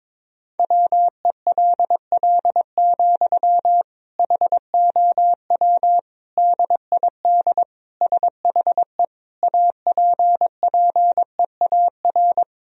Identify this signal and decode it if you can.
Morse code, heard as a keyed tone, about 22 wpm, 700 Hz. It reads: WELL, HOW DID SHE APPEAR
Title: Sentences from Top 1000 Words - Review Part 2 - 22wpm